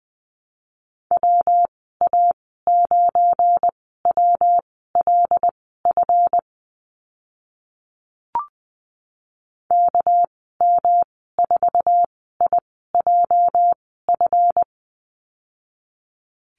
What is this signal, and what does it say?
Morse code, heard as a keyed tone, about 20 wpm, spaced as for 15 wpm, 700 Hz. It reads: WA9WLF KM4IJF
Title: 219 Lesson 35g - Number 0 - Call Signs 20wpm